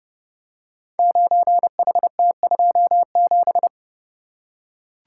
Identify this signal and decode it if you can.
Morse code, heard as a keyed tone, about 30 wpm, 700 Hz. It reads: 9HT27
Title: Code Groups - 5 Letter:Numbers - 30wpm